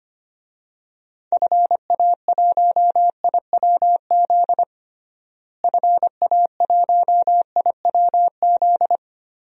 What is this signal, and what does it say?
Morse code, heard as a keyed tone, about 25 wpm, 700 Hz. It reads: FA1IWZ FA1IWZ
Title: DXCC-100 Call Signs - 6 Characters - Repeated 3x - 25wpm